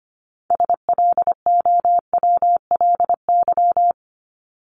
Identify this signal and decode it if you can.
Morse code, heard as a keyed tone, about 25 wpm, 700 Hz. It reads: SLOWLY
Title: Top 1001-1250 Words 25wpm